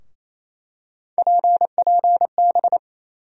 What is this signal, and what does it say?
Morse code, heard as a keyed tone, about 28 wpm, 700 Hz. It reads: PPB
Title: USA Call Sign Postfixes - 3 Letters 28wpm